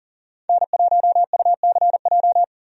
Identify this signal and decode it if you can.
Morse code, heard as a keyed tone, about 40 wpm, 700 Hz. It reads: N1UCJ
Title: USA Call Signs - 5 Characters - 40wpm